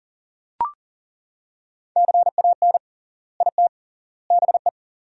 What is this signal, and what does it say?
Morse code, heard as a keyed tone, about 40 wpm, 700 Hz. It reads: CAN IT BE
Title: 3x Word-Spacing - Sentences from Top 500 Words - 40wpm